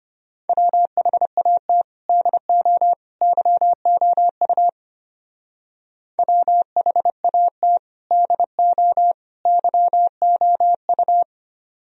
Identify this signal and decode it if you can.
Morse code, heard as a keyed tone, about 30 wpm, 700 Hz. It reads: WHAT DO YOU WHAT DO YOU
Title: Speed-Racing - Sets of 3 Words 25wpm